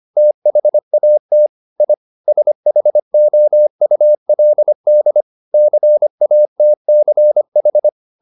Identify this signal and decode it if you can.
Morse code, heard as a keyed tone, about 25 wpm, 600 Hz. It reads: THAT I SHOULD CATCH
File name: the old man and the sea part 1 25wpm